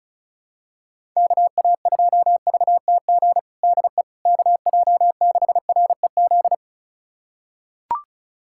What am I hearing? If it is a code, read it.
Morse code, heard as a keyed tone, about 36 wpm, 700 Hz. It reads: KA2VTG DE KJ6REZ
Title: USA Callsign DE Callsign - 6 Characters - 35wpm